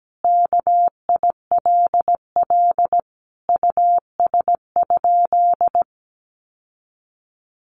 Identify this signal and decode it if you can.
Morse code, heard as a keyed tone, about 17 wpm, 700 Hz. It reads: KILL US?
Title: Sentences from Top 1400 Words 17wpm